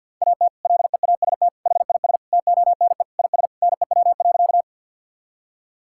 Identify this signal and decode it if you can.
Morse code, heard as a keyed tone, about 50 wpm, 700 Hz. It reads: AT LEAST HIS TONE IS NEW.
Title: Sentences from Top 1600 Words 50wpm